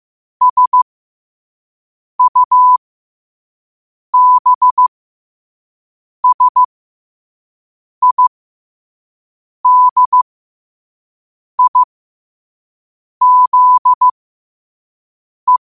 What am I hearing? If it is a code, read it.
Morse code, heard as a keyed tone, about 15 wpm, spaced as for 5 wpm, 1000 Hz. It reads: SUBSIDIZE
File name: qotdfiles / 5wpm